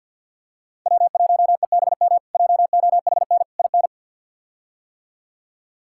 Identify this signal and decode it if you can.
Morse code, heard as a keyed tone, about 47 wpm, 700 Hz. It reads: W1EBM JOHN IN
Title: K1USN SST QSO Practice - 50wpm